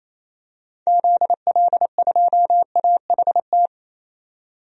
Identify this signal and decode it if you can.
Morse code, heard as a keyed tone, about 28 wpm, 700 Hz. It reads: ZL2AHT
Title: DXCC-100 Call Signs - 6 Characters - 28wpm